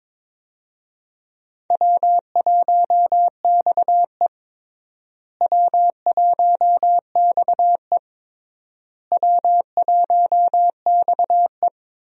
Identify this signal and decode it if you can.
Morse code, heard as a keyed tone, about 22 wpm, 700 Hz. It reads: W1XE W1XE W1XE
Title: USA Call Signs - 4 Characters - Repeated 3x - 22wpm